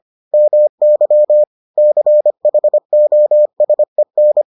Morse code, heard as a keyed tone, about 25 wpm, 600 Hz. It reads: MY CHOSEN